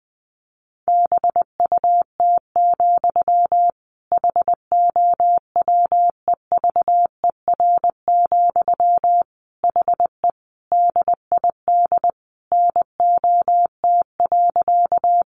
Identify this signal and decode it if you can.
Morse code, heard as a keyed tone, about 20 wpm, 700 Hz. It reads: BUT, HOWEVER, HE DID NOT.
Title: Sentences from Top 600 Words 20wpm